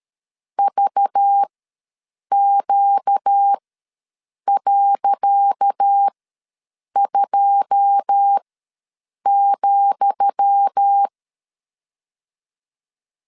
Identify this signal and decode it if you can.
Morse code, heard as a keyed tone, about 13 wpm, spaced as for 7 wpm, 800 Hz. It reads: VQ.2,
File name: LEZIONI / 10 SIGNS / 10D COMA ws 12wpm 3618